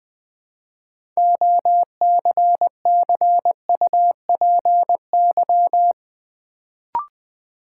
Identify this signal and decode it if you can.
Morse code, heard as a keyed tone, about 20 wpm, 700 Hz. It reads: OCCUPY
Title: Build A Word - 2 Letters At A Time - 6-Letter Words - 20wpm